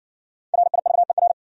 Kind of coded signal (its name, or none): Morse code